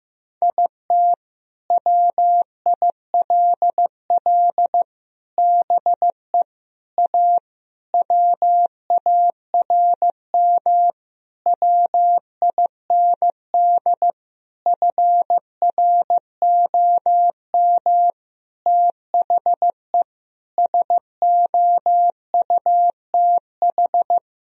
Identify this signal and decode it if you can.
Morse code, heard as a keyed tone, about 15 wpm, 700 Hz. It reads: IT WILL BE A WARM WIND FROM THE SOUTH
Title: Sentences from Top 1000 Words - Review Part 3 - 15wpm